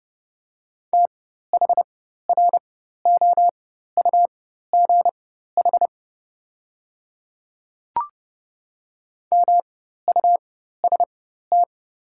Morse code, heard as a keyed tone, about 30 wpm, spaced as for 14 wpm, 700 Hz. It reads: THROUGH MUST
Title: 4x Character Spacing - Top 200 Words - 30wpm